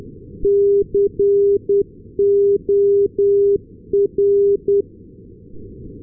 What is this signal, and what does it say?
Morse code, heard as a keyed tone, about 10 wpm, 400 Hz. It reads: COR
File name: ndb corner cor-388